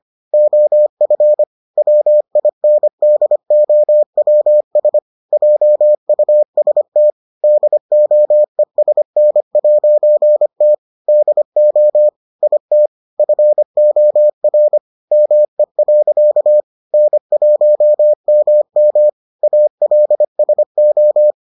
Morse code, heard as a keyed tone, about 25 wpm, 600 Hz. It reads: OF WINDOWS JUST DOESN'T DO IT FOR ME. N1MM ALSO